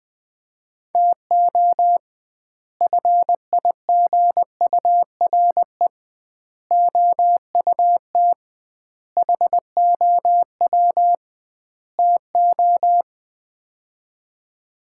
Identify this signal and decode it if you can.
Morse code, heard as a keyed tone, about 19 wpm, 700 Hz. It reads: TO FIGURE OUT HOW TO
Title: 2x Word-Spacing - Sets of 5 Words - 20wpm